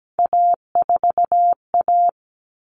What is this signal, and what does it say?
Morse code, heard as a keyed tone, about 17 wpm, 700 Hz. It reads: A4A
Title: All Cut Numbers - 3-Digit 17wpm